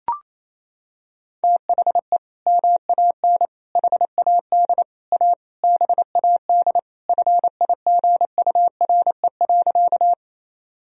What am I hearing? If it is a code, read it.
Morse code, heard as a keyed tone, about 28 wpm, 700 Hz. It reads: THE MAN HAD A BAD FIGURE.
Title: Sentences from Top 1000 Words - Review Part 2 - 28wpm